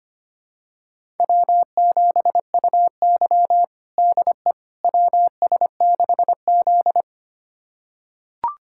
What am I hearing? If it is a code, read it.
Morse code, heard as a keyed tone, about 25 wpm, 700 Hz. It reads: W7UY DE WS6Z
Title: USA Callsign DE Callsign - 4 Characters - 25wpm